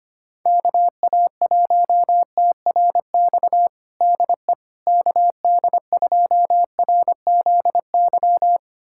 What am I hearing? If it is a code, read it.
Morse code, heard as a keyed tone, about 25 wpm, 700 Hz. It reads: KA1TRX DE KD2RZY
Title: USA Callsign DE Callsign - 6 Characters - 25wpm